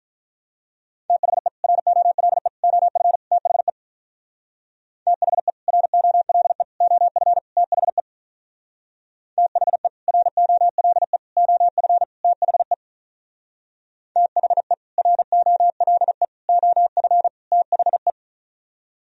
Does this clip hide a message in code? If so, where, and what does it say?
Yes: Morse code from 1.1 to 18.1 s, heard as a keyed tone, about 44 wpm, 700 Hz. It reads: THE ROLE OF THE THE ROLE OF THE THE ROLE OF THE THE ROLE OF THE